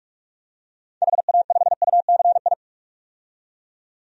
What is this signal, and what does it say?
Morse code, heard as a keyed tone, about 46 wpm, 700 Hz. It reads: HA5UCI
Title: DXCC-100 Call Signs - 6 Characters - 45wpm